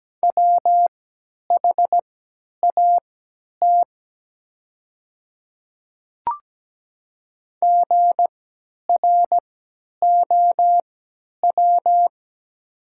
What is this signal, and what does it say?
Morse code, heard as a keyed tone, about 17 wpm, spaced as for 10 wpm, 700 Hz. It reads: WHAT GROW
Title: 3x Character Spacing - Top 500 Words - 17wpm